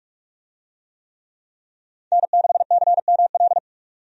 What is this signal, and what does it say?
Morse code, heard as a keyed tone, about 46 wpm, 700 Hz. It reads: N6CML
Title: USA Call Signs - 5 Characters - 45wpm